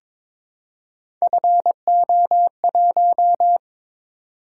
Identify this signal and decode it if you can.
Morse code, heard as a keyed tone, about 22 wpm, 700 Hz. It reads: FO1
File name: practice / 22wpm / Three Letter-Number Combinations - 22wpm